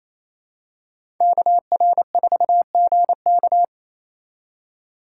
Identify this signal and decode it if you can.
Morse code, heard as a keyed tone, about 28 wpm, 700 Hz. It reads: KR4GK
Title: USA Call Signs - 2x Character Spacing - 28wpm